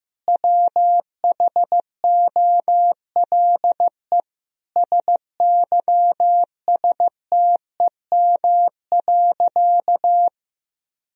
Morse code, heard as a keyed tone, about 15 wpm, 700 Hz. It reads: WHOLE SYSTEM.
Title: Sentences from Top 1000 Words - Review Part 3 - 15wpm